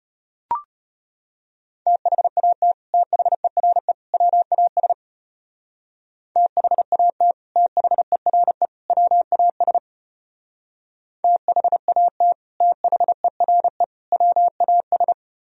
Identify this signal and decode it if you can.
Morse code, heard as a keyed tone, about 34 wpm, 700 Hz. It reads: THAT THERE WAS THAT THERE WAS THAT THERE WAS